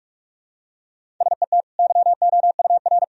Morse code, heard as a keyed tone, about 46 wpm, 700 Hz. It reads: SET YOUR
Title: Sentences from Top 1000 Words - Review Part 1 - 45wpm